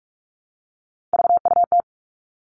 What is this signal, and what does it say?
Morse code, heard as a keyed tone, about 44 wpm, 700 Hz. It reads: VUT
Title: All Cut Numbers - 3-Digit 45wpm